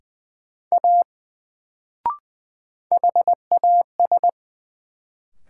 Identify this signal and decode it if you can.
Morse code, heard as a keyed tone, about 20 wpm, 700 Hz. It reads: A HAS
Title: Sentences from Top 100 Words - New Word Drill - 20wpm